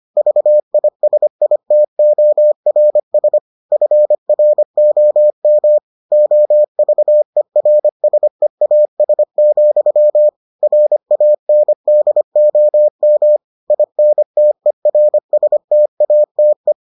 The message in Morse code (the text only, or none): VISITORS FROM OVERSEAS, RANDOM INTERSTATE